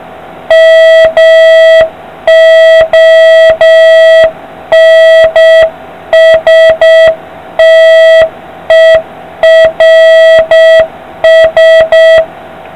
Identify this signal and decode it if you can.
Morse code, heard as a keyed tone, about 7 wpm, 650 Hz, with noise. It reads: MONSTERS